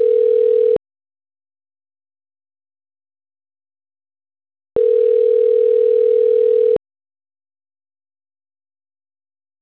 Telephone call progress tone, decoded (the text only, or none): ringback tone